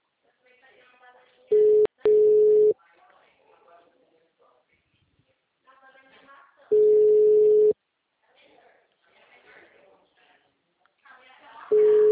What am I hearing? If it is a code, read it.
Ringback tone (Europe, 425 Hz), with speech.